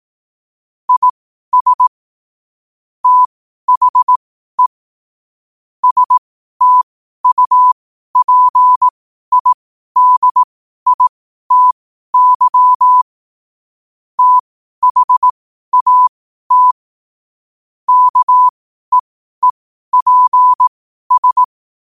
Morse code, heard as a keyed tone, about 18 wpm, spaced as for 12 wpm, 1000 Hz. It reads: IS THE STUPIDITY THAT KEEPS
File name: qotdfiles / 12wpm